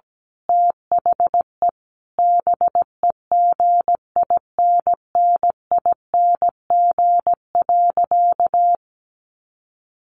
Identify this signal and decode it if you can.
Morse code, heard as a keyed tone, about 17 wpm, 700 Hz. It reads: THE BEGINNING.
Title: A Sound Of Thunder 17wpm